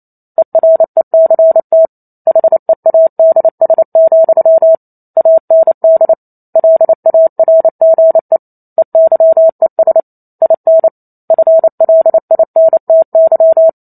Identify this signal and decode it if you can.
Morse code, heard as a keyed tone, about 28 wpm, 650 Hz. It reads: ERECT HEADS, AND LARGE EYES IN FLINTY